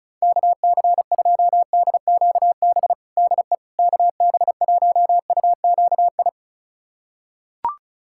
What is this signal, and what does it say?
Morse code, heard as a keyed tone, about 36 wpm, 700 Hz. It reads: KC2DQB DE KB1UQI
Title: USA Callsign DE Callsign 35wpm